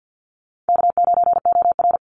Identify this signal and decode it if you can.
Morse code, heard as a keyed tone, about 50 wpm, 700 Hz. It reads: K9OR
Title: Easy Call Signs Repeated 2x - 50wpm